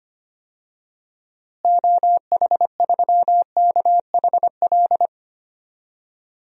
Morse code, heard as a keyed tone, about 25 wpm, 700 Hz. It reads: OH3KHL
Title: DXCC-100 Call Signs - 6 Characters - Repeated 3x - 25wpm